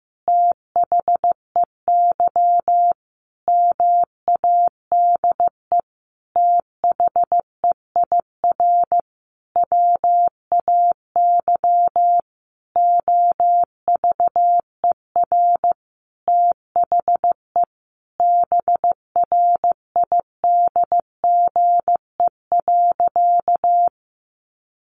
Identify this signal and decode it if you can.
Morse code, heard as a keyed tone, about 15 wpm, 700 Hz. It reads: THEY MADE THEIR WAY OVER THE BRIDGE.
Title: Sentences from Top 1500 Words 15wpm